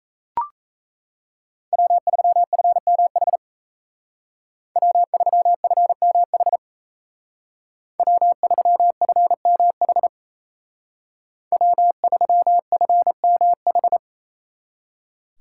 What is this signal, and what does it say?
Morse code, heard as a keyed tone, about 33 wpm, 700 Hz. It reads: W3FMH W3FMH W3FMH W3FMH
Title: Speed Racing - USA 1x3 Call Signs 28wpm